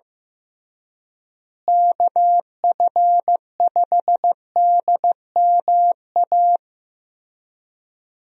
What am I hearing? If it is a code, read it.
Morse code, heard as a keyed tone, about 15 wpm, 700 Hz. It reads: KF5DMA
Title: USA Call Signs - 6 Characters - Repeated 3x - 15wpm